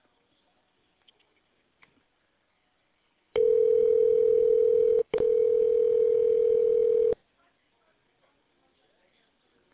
Ringback tone (North America), 440 + 480 Hz.